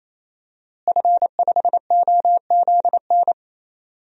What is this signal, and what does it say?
Morse code, heard as a keyed tone, about 28 wpm, 700 Hz. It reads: F5OZN
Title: DXCC-100 Call Signs - 5 Characters - Repeated 3x - 28wpm